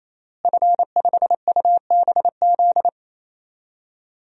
Morse code, heard as a keyed tone, about 27 wpm, 700 Hz. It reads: F5UBZ
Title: DXCC-100 Call Signs - 5 Characters - Repeated 3x - 28wpm